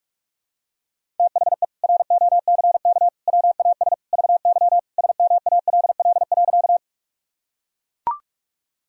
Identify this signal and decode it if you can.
Morse code, heard as a keyed tone, about 41 wpm, 700 Hz. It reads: THE ROCK WAS VY SMALL.
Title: Sentences from Top 400 Words and Abbreviations 45wpm